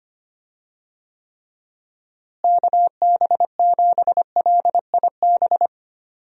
Morse code, heard as a keyed tone, about 25 wpm, 700 Hz. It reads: KB7LIB
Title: USA Call Signs - 6 Characters - Repeated 2x - 25wpm